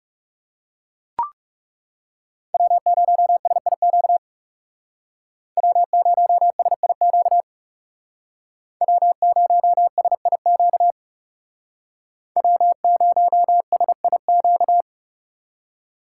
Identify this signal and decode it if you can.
Morse code, heard as a keyed tone, about 36 wpm, 700 Hz. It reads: W0SIQ W0SIQ W0SIQ W0SIQ
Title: Speed Racing - USA 1x3 Call Signs 30wpm